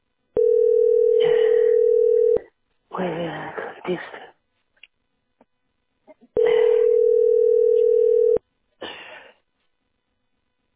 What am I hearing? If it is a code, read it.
Ringback tone (North America), 440 + 480 Hz.